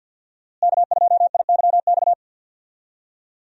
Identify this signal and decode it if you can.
Morse code, heard as a keyed tone, about 49 wpm, 700 Hz. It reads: K2IYX
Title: USA Call Signs - 5 Characters - Repeated 2x - 50wpm